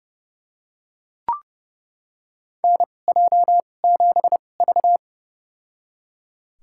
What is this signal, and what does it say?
Morse code, heard as a keyed tone, about 30 wpm, spaced as for 22 wpm, 700 Hz. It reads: NJ7V
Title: USA Call Signs - 4 Characters - 2x Character Spacing - 30wpm